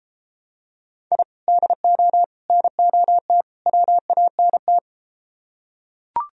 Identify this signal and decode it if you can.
Morse code, heard as a keyed tone, about 33 wpm, 700 Hz. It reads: I DO NOT WANT